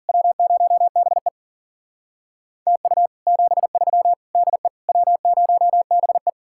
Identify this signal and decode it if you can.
Morse code, heard as a keyed tone, about 50 wpm, 700 Hz. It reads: W0BE TU 73 DE W0BE
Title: Speed-Racing Common QSO Phrases Encore - 40wpm